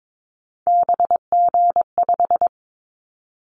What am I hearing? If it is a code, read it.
Morse code, heard as a keyed tone, about 22 wpm, 700 Hz. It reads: BG5
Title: Callsign Prefixes for Top 100 DXCC - 22wpm